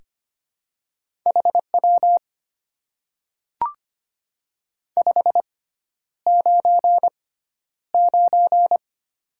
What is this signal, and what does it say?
Morse code, heard as a keyed tone, about 25 wpm, spaced as for 11 wpm, 700 Hz. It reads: HW 599
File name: Top 90 QSO Elements - 6x Character Spacing - 25wpm